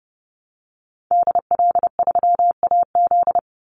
Morse code, heard as a keyed tone, about 30 wpm, 700 Hz. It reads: DL3AZ
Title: Call Signs - 30wpm